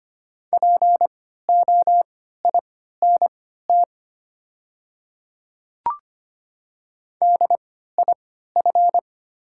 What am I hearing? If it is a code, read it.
Morse code, heard as a keyed tone, about 25 wpm, spaced as for 14 wpm, 700 Hz. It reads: POINT DIF
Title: 3x Character Spacing - Top 200 Words - 25wpm